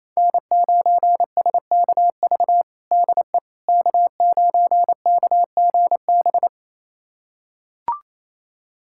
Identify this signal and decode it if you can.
Morse code, heard as a keyed tone, about 28 wpm, 700 Hz. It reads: N9SKV DE K9KGB
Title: USA Callsign DE Callsign - 5 Characters - 28wpm